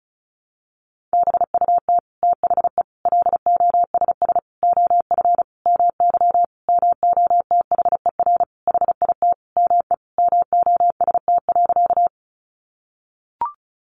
Morse code, heard as a keyed tone, about 34 wpm, 700 Hz. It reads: BUT THE LOSS OF MY MOTHER HIT ME MOST.
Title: Sentences from Top 1400 Words 35wpm